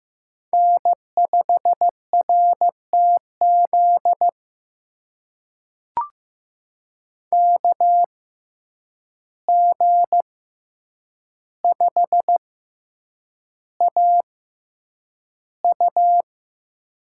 Morse code, heard as a keyed tone, about 15 wpm, spaced as for 8 wpm, 700 Hz. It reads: N5RTZ KG5AU
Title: USA Call Signs - 5 Characters - 6x Character Spacing - 15wpm